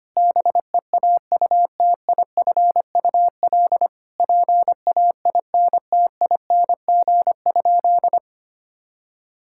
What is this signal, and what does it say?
Morse code, heard as a keyed tone, about 25 wpm, 700 Hz. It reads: BEAUTIFUL PAINTING?
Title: Sentences from Top 1000 Words - Review Part 6 - 25wpm